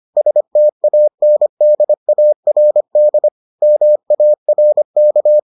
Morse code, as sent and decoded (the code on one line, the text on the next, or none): ... - .- -. -.. .- .-. -.. / -- .- .-. -.-
STANDARD MARK